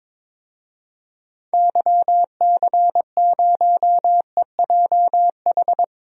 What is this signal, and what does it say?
Morse code, heard as a keyed tone, about 22 wpm, 700 Hz. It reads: YC0EJH
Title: DXCC-100 Call Signs - 6 Characters - Repeated 3x - 22wpm